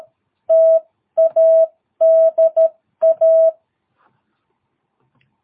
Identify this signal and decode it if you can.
Morse code, heard as a keyed tone, about 13 wpm, 660 Hz. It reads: TADA
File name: jh8cbh / 2022cw